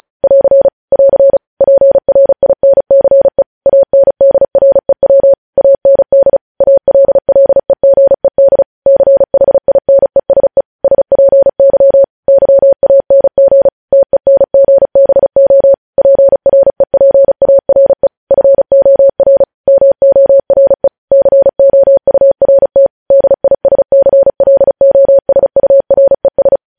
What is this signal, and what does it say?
Morse code, heard as a keyed tone, about 36 wpm, 550 Hz. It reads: <AR> <AR> PRINCE ANDREW AND ALLEGED CHINESE SPY YANG TENGBO PREPARE FOR MORE COURT DISCLOSURES
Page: news headlines in morse code at 35 wpm